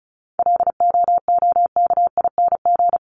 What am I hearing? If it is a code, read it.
Morse code, heard as a keyed tone, about 34 wpm, 700 Hz. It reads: LOOKING
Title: ING Suffix - Top 100 Words with Base Words 35wpm